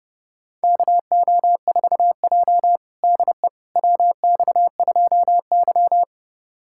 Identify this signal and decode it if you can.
Morse code, heard as a keyed tone, about 32 wpm, 700 Hz. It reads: KO4J DE WX2Y
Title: USA Callsign DE Callsign - 4 Characters - 30wpm